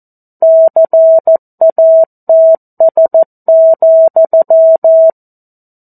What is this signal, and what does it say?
Morse code, heard as a keyed tone, about 14 wpm, 650 Hz. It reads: CATS,
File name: 02-Chapter-02 14 WPM